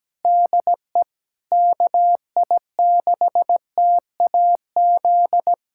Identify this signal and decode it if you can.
Morse code, heard as a keyed tone, about 17 wpm, 700 Hz. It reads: DE KI6TAZ
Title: USA Callsign DE Callsign 17wpm